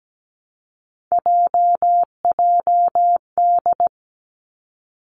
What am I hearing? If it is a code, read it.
Morse code, heard as a keyed tone, about 17 wpm, 700 Hz. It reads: JJD
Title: Three Challenging Letter-Number Combinations 17wpm